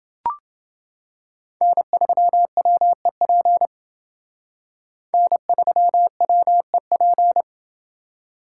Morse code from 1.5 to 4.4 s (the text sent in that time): N3WEP